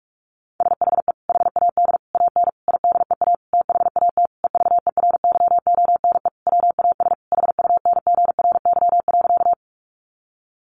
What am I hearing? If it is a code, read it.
Morse code, heard as a keyed tone, about 44 wpm, 700 Hz. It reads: SHE HAD AN IDEA THAT EVERYONE WAS HUNGRY.
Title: Sentences from Top 1400 Words 45wpm